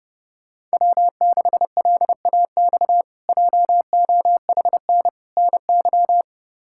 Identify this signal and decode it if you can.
Morse code, heard as a keyed tone, about 30 wpm, 700 Hz. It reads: W6LAX JOHN NY
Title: K1USN SST QSO Practice - 30wpm